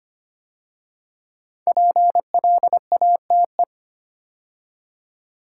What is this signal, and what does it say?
Morse code, heard as a keyed tone, about 25 wpm, 700 Hz. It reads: PLATE
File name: 5 Letter Words - First 3 Letters Given - 25wpm